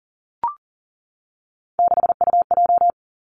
Morse code, heard as a keyed tone, about 40 wpm, 700 Hz. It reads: 6UJ
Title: Three Challenging Letter-Number Combinations 40wpm